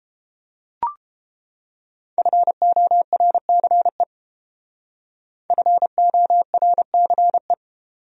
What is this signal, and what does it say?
Morse code, heard as a keyed tone, about 30 wpm, 700 Hz. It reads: FORCE FORCE